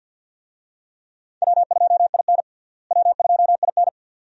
Morse code, heard as a keyed tone, about 52 wpm, 700 Hz. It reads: W2IN W2IN
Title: USA Call Signs - 4 Characters - Repeated 2x - 50wpm